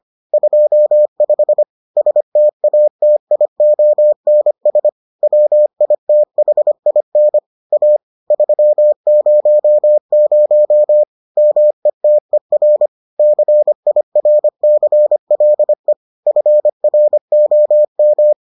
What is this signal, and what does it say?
Morse code, heard as a keyed tone, about 25 wpm, 600 Hz. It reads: 25 STATIONS WITHIN A 300 METER CIRCLE FROM